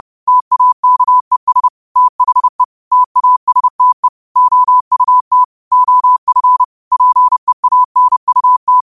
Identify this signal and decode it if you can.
Morse code, heard as a keyed tone, about 30 wpm, 1000 Hz. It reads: TAKES THE TASTE OUT OF PEANUT